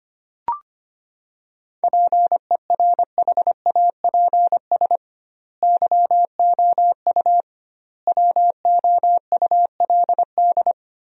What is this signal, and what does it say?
Morse code, heard as a keyed tone, about 25 wpm, 700 Hz. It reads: PERHAPS YOU WOULD
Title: 2x Word-Spacing - Sentences from Top 500 Words - 25wpm